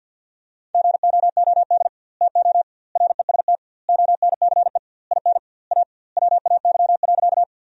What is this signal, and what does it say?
Morse code, heard as a keyed tone, about 52 wpm, 700 Hz. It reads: GOOD TO REST ONCE IN A WAY.
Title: 2x Word-Spacing - Sentences from Top 500 Words - 50wpm